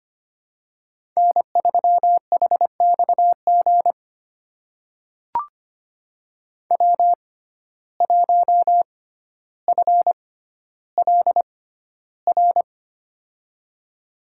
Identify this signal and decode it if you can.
Morse code, heard as a keyed tone, about 25 wpm, spaced as for 13 wpm, 700 Hz. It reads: N3HXG W1FLR
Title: USA Call Signs - 5 Characters - 6x Character Spacing - 25wpm